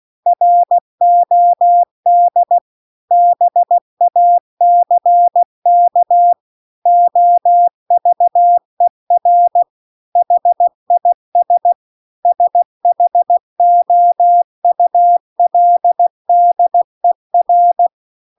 Morse code, heard as a keyed tone, about 16 wpm, 700 Hz. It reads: ROD BACK OVER HIS SHOULDER
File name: CW-The-Big-Twoheaded-River-2-16WPM